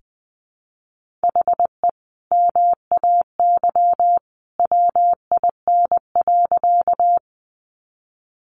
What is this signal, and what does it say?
Morse code, heard as a keyed tone, about 20 wpm, 700 Hz. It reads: HE MAY WIN.
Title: Sentences from Top 1500 Words 20wpm